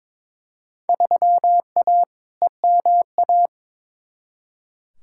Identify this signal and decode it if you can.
Morse code, heard as a keyed tone, about 21 wpm, 700 Hz. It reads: 3A EMA
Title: Field Day Exchanges - 22wpm